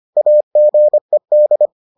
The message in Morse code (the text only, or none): AGED